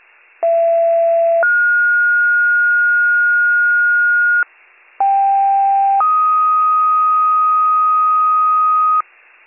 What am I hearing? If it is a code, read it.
Two-tone page, heard as a one-second tone, then a three-second tone, about 670 then 1430 Hz, about 790 then 1250 Hz, with noise. It is heard from 0.4 to 9.0 s.